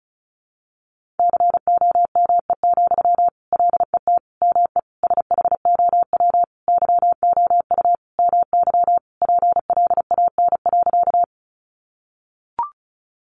Morse code, heard as a keyed tone, about 34 wpm, 700 Hz. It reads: COME, LET ME SHOW YOU MY PLAN.
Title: Sentences from Top 500 Words 35wpm